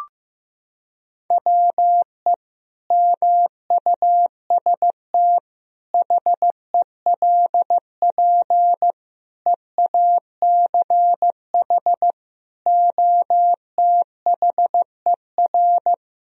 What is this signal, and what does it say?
Morse code, heard as a keyed tone, about 15 wpm, 700 Hz. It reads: WE MUST HELP EACH OTHER
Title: Sentences from Top 1000 Words - Review Part 1 - 15wpm